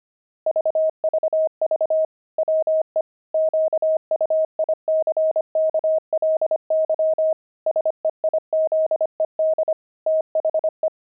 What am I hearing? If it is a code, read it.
Morse code, heard as a keyed tone, about 25 wpm, 630 Hz. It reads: VVV WE QUICKLY SEIZED THE
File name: pangram-fs-25-cs-25-axle